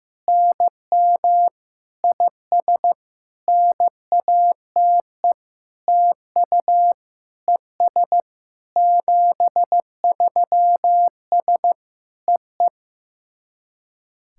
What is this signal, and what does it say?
Morse code, heard as a keyed tone, about 15 wpm, 700 Hz. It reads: NM IS NATE TU ES 73S EE
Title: SOTA QSO Practice 15wpm